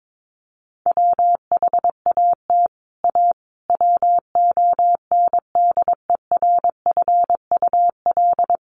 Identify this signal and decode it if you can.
Morse code, heard as a keyed tone, about 22 wpm, 700 Hz. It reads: WHAT A WONDERFUL
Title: Sentences from Top 1100 Words 22wpm